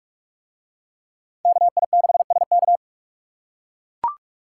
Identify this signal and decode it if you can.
Morse code, heard as a keyed tone, about 44 wpm, 700 Hz. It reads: KI6SK